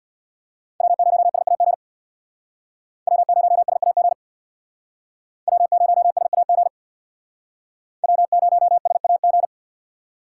Morse code, heard as a keyed tone, about 63 wpm, 690 Hz. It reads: W0SAG W0SAG W0SAG W0SAG